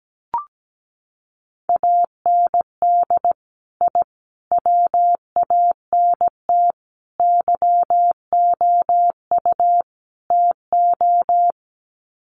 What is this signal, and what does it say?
Morse code, heard as a keyed tone, about 17 wpm, 700 Hz. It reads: AND I WANT YOU TO
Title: Sets of 5 Words Encore 17wpm